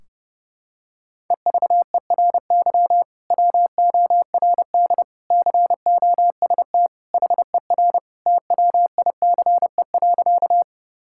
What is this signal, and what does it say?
Morse code, heard as a keyed tone, about 30 wpm, 700 Hz. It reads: EVERY WORD COST HER TWICE.
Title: Sentences from Top 1600 Words 30wpm